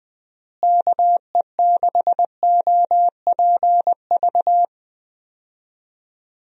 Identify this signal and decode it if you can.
Morse code, heard as a keyed tone, about 20 wpm, 700 Hz. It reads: KE6OPV
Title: USA Call Signs - 6 Characters - Repeated 2x - 20wpm